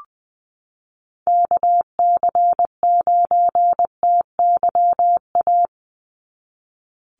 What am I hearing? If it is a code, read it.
Morse code, heard as a keyed tone, about 20 wpm, 700 Hz. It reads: KC9TYA
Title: NA SOTA Chaser Callsigns 20wpm